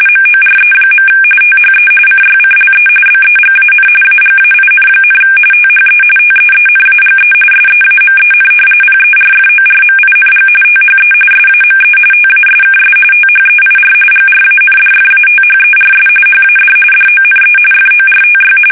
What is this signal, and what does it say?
RTTY, about 45.45 baud. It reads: MORE. THE CYCLE BEGAN AGAIN, THE TEENAGE SEAMEN PERFORMING THE SAME TASKS THEIR GRANDFATHERS HAD DONE FORTY
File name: fsk850 / battleshipnj